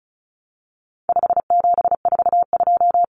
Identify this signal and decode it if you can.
Morse code, heard as a keyed tone, about 34 wpm, 700 Hz. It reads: 5742
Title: Common Cut Numbers - 4-Digit 35wpm